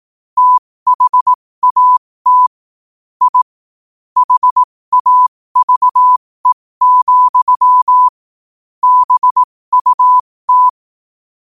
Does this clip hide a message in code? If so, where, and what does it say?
Yes: Morse code from 0.4 to 10.7 s, heard as a keyed tone, about 18 wpm, spaced as for 15 wpm, 1000 Hz. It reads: THAT I HAVE, BUT